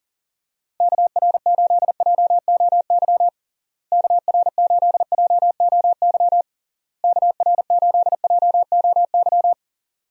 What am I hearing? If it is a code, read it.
Morse code, heard as a keyed tone, about 40 wpm, 700 Hz. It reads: KR8JOY KR8JOY KR8JOY
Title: USA Call Signs - 6 Characters - Repeated 3x - 40wpm